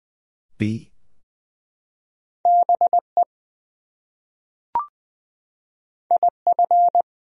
Morse code, heard as a keyed tone, about 20 wpm, 700 Hz. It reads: BE IF
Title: Top 100 Words Repeated 20wpm